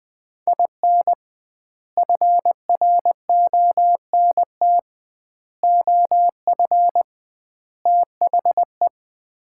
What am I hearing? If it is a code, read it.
Morse code, heard as a keyed tone, about 21 wpm, 700 Hz. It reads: IN FRONT OF THE
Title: 2x Word-Spacing - Sets of 4 Words - 20wpm